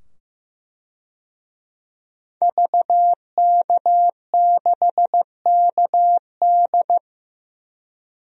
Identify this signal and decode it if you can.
Morse code, heard as a keyed tone, about 15 wpm, 700 Hz. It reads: VK6KD
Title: DXCC-100 Call Signs - 5 Characters - 15wpm